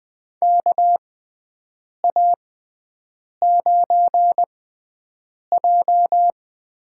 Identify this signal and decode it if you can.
Morse code, heard as a keyed tone, about 20 wpm, spaced as for 7 wpm, 700 Hz. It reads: KA9J
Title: USA Call Signs - 6x Character Spacing - 20wpm